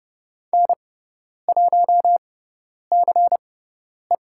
Morse code, heard as a keyed tone, about 30 wpm, spaced as for 10 wpm, 700 Hz. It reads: N1CE
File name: Lesson 16g - Letter c - Call Signs 30wpm